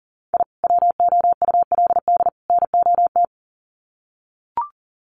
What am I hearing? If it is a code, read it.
Morse code, heard as a keyed tone, about 40 wpm, 700 Hz. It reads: I WOULD NOT
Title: Sets of 3 Words 40wpm